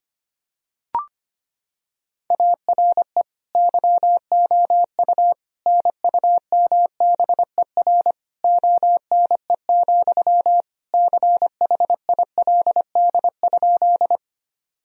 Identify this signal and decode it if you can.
Morse code, heard as a keyed tone, about 25 wpm, 700 Hz. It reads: ARE YOU NUMBER ONE, CHILD?
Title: Sentences from Top 1000 Words - Review Part 1 - 25wpm